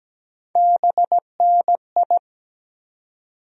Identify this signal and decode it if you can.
Morse code, heard as a keyed tone, about 17 wpm, 700 Hz. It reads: BNI
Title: USA Call Sign Postfixes - 3 Letters 17wpm